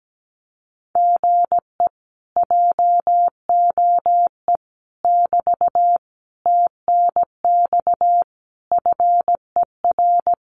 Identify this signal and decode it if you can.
Morse code, heard as a keyed tone, about 17 wpm, 700 Hz. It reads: GE JOE <BT> TNX FER
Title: Common QSO Phrases Encore 17wpm